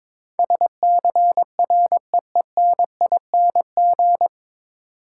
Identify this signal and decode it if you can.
Morse code, heard as a keyed tone, about 22 wpm, 700 Hz. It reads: SCREENING
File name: 9 Letter Words - First 3 Letters Given - 22wpm